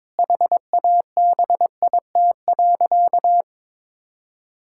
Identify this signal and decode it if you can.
Morse code, heard as a keyed tone, about 22 wpm, 700 Hz. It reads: HABIT.
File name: Difficult Sentences with Hints - 22wpm